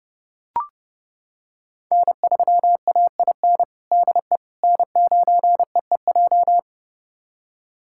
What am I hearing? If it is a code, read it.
Morse code, heard as a keyed tone, about 30 wpm, 700 Hz. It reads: N3AIN DE N9EEJ
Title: USA Callsign DE Callsign 30wpm